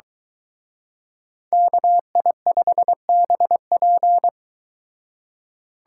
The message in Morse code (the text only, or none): KI5BP